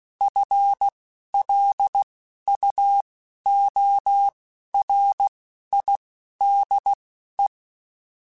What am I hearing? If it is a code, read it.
Morse code, heard as a keyed tone, about 16 wpm, spaced as for 11 wpm, 800 Hz. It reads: FLUORIDE